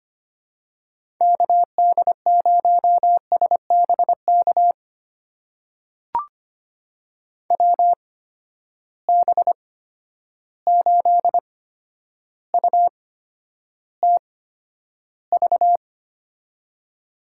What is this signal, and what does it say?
Morse code, heard as a keyed tone, about 25 wpm, spaced as for 11 wpm, 700 Hz. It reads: KD0SBK WB8UTV
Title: USA Call Signs - 6 Characters - 8x Character Spacing - 25wpm